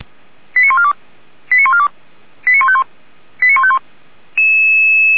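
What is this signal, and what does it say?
Five-tone selective call (ZVEI), heard as quick runs of five tones, tones of about 70 ms. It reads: 89142 89142 89151 89151
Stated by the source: ZVEI five-tone call received by SDR, logged by Feuerwehr Orlamünde (f85360).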